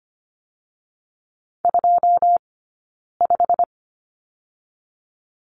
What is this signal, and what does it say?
Morse code, heard as a keyed tone, about 25 wpm, spaced as for 9 wpm, 700 Hz. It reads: 25
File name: Lesson 17d - Letter u - Two Characters 25wpm